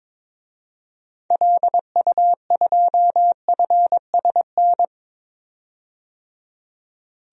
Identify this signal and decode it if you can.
Morse code, heard as a keyed tone, about 22 wpm, 700 Hz. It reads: LU2FSN
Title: DXCC-100 Call Signs - 6 Characters - 22wpm